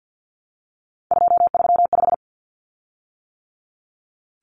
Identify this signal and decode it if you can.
Morse code, heard as a keyed tone, about 50 wpm, 700 Hz. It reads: 235
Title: Common Cut Numbers - 3-Digit 50wpm